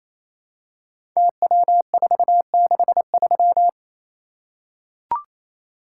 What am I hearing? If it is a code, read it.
Morse code, heard as a keyed tone, about 27 wpm, 700 Hz. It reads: TW463
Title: Code Groups - 5 Letter:Numbers - 28wpm